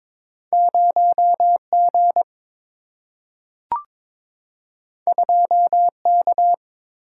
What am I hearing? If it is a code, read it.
Morse code, heard as a keyed tone, about 22 wpm, 700 Hz. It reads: 0G 2K
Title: Single-Letter Single-Number 22wpm